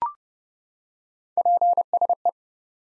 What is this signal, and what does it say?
Morse code, heard as a keyed tone, about 30 wpm, 700 Hz. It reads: PSE
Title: Top 90 Abbr 30WPM